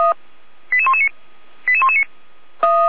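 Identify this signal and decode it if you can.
Five-tone selective call (ZVEI), heard as quick runs of five tones, tones of about 70 ms. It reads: 88119 88119
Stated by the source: ZVEI five-tone call 88125 received on SDR.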